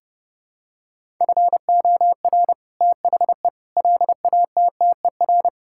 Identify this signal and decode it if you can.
Morse code, heard as a keyed tone, about 30 wpm, 700 Hz. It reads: FOR THE LATTER
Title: Sentences from Top 1600 Words 30wpm